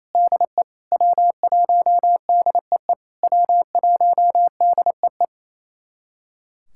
Morse code, heard as a keyed tone, about 28 wpm, 700 Hz. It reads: DE W1DEE W1DEE
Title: SOTA QSO Practice 28wpm